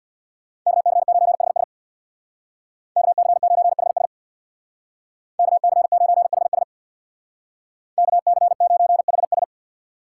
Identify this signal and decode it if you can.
Morse code, heard as a keyed tone, about 64 wpm, 700 Hz. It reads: KC9HS KC9HS KC9HS KC9HS